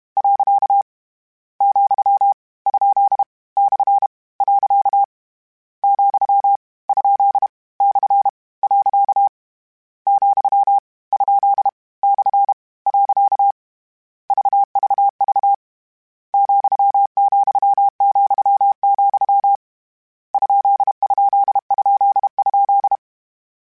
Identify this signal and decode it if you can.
Morse code, heard as a keyed tone, about 32 wpm, 800 Hz. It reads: . , ? / . , ? / . , ? / . VVV ,,,, ????
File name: Misc / Misc-mp3s / Punct @30